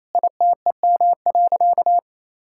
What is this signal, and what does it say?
Morse code, heard as a keyed tone, about 27 wpm, 700 Hz. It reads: ITEM.
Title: Sentences from Top 1000 Words - Review Part 6 - 28wpm